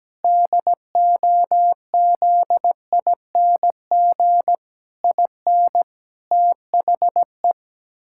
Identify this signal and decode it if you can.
Morse code, heard as a keyed tone, about 17 wpm, 700 Hz. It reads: DOZING IN THE
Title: Pangram Challenge v2 - 17wpm